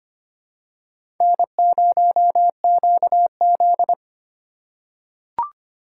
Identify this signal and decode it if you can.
Morse code, heard as a keyed tone, about 25 wpm, 700 Hz. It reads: N0QZ
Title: USA Call Signs - 4 Characters - 2x Character Spacing - 25wpm